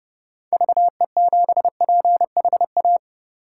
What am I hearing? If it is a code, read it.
Morse code, heard as a keyed tone, about 32 wpm, 700 Hz. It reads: VE7PHA
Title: K1USN SST QSO Practice - 30wpm